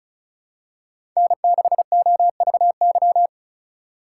Morse code, heard as a keyed tone, about 36 wpm, 700 Hz. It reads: N6OVY
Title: USA Call Signs - 5 Characters - 4x Character Spacing - 35wpm